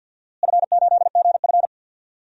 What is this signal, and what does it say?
Morse code, heard as a keyed tone, about 47 wpm, 700 Hz. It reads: F8GF
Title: DXCC-100 Call Signs - 4 Characters - Repeated 2x - 50wpm